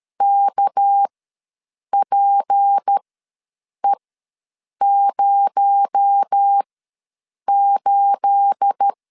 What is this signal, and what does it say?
Morse code, heard as a keyed tone, about 13 wpm, spaced as for 7 wpm, 800 Hz. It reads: KPE08
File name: LEZIONI / 9 28 ws 12wpm 3354